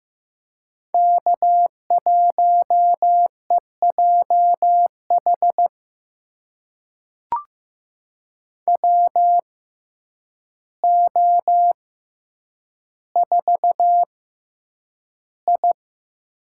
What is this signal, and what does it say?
Morse code, heard as a keyed tone, about 15 wpm, spaced as for 8 wpm, 700 Hz. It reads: K1EJH WO4I
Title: USA Call Signs - 6x Character Spacing - 15wpm